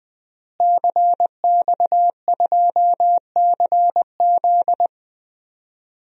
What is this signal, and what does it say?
Morse code, heard as a keyed tone, about 21 wpm, 700 Hz. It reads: CX2CZ
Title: DXCC-100 Call Signs - 5 Characters - Repeated 3x - 20wpm